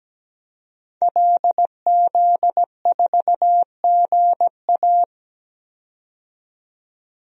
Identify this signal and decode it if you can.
Morse code, heard as a keyed tone, about 17 wpm, 700 Hz. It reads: LZ4GA
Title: DXCC-100 Call Signs - 5 Characters - 17wpm